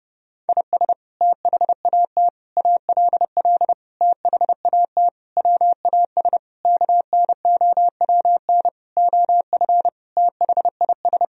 Morse code, heard as a keyed tone, about 32 wpm, 700 Hz. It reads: IS THAT ALL THAT WAS KNOWN OF THIS
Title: Sentences from Top 1000 Words - Review Part 3 - 30wpm